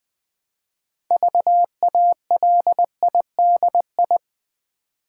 Morse code, heard as a keyed tone, about 20 wpm, 700 Hz. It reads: VALIDI
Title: Build A Word - 3 Letters At A Time - 8-Letter Words - 20wpm